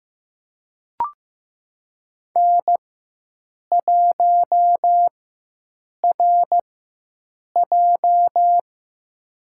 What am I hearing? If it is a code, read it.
Morse code, heard as a keyed tone, about 15 wpm, spaced as for 7 wpm, 700 Hz. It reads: N1RJ
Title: USA Call Signs - 4 Characters - 4x Character Spacing - 15wpm